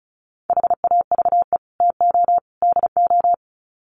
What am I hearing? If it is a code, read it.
Morse code, heard as a keyed tone, about 34 wpm, 700 Hz. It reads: HAVE TO DO
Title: Sets of 3 Words 35wpm